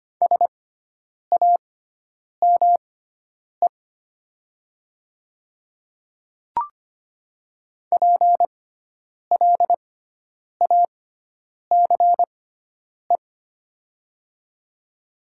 Morse code, heard as a keyed tone, about 25 wpm, spaced as for 9 wpm, 700 Hz. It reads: SAME PLACE
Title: 6x Character Spacing - Top 300 Words - 25wpm